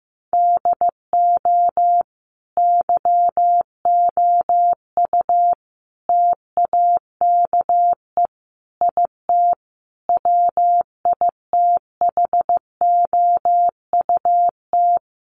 Morse code, heard as a keyed tone, about 15 wpm, 700 Hz. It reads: DO YOU TAKE IT WITHOUT
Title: Sentences from Top 1200 Words 15wpm